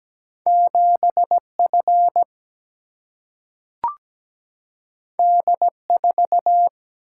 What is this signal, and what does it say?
Morse code, heard as a keyed tone, about 17 wpm, 700 Hz. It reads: 7F D4
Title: Single-Letter Single-Number 17wpm